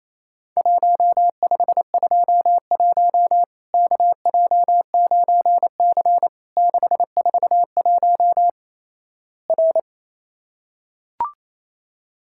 Morse code, heard as a keyed tone, about 27 wpm, 700 Hz. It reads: 1521 KJ9C 641 R
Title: Worked All Europe DX Contest - QTC Exchange - 28wpm